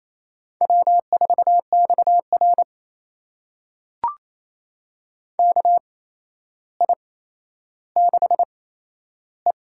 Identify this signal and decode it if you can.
Morse code, heard as a keyed tone, about 28 wpm, spaced as for 12 wpm, 700 Hz. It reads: W4XR KI6E
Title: USA Call Signs - 8x Character Spacing - 28wpm